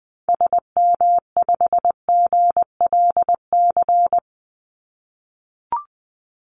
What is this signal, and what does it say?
Morse code, heard as a keyed tone, about 20 wpm, 700 Hz. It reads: SM5GLC
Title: Call Signs - 20wpm